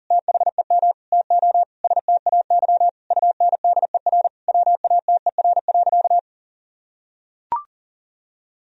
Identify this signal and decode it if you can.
Morse code, heard as a keyed tone, about 40 wpm, 700 Hz. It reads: THEM TO STAY UNDER WATER.
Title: Sentences from Top 1000 Words - Review Part 2 - 40wpm